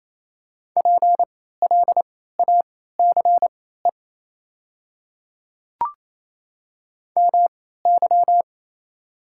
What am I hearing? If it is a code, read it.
Morse code, heard as a keyed tone, about 28 wpm, spaced as for 16 wpm, 690 Hz. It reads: PLACE MY
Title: 3x Character Spacing - Top 200 Words - 28wpm